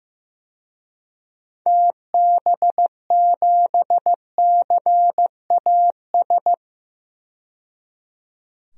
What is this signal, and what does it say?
Morse code, heard as a keyed tone, about 15 wpm, 700 Hz. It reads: TB7CAS
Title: DXCC-100 Call Signs - 6 Characters - 15wpm